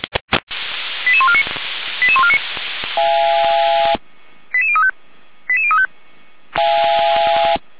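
Five-tone selective call (ZVEI), heard as quick runs of five tones, tones of about 70 ms. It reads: 88149 88149 88035 88035
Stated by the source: ZVEI five-tone selective call 88074 received with an SDR, logged by the Orlamünde fire brigade (feuerwehr-orlamuende).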